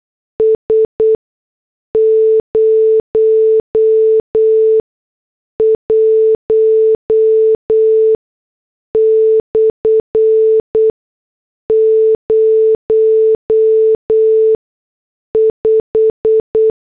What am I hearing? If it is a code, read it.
Morse code, heard as a keyed tone, about 8 wpm, spaced as for 6 wpm, 440 Hz. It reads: S 01/05
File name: morse / headlines